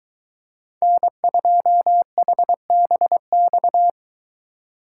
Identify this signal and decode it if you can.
Morse code, heard as a keyed tone, about 23 wpm, 700 Hz. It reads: N2HBX